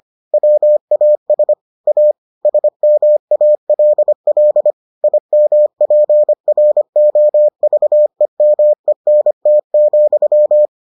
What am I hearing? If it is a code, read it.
Morse code, heard as a keyed tone, about 25 wpm, 600 Hz. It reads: WAS A SMALL IMPROVEMENT,